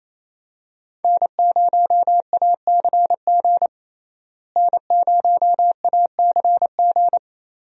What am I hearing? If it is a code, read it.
Morse code, heard as a keyed tone, about 28 wpm, 700 Hz. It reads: N0ACG N0ACG
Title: USA Call Signs - 5 Characters - Repeated 3x - 28wpm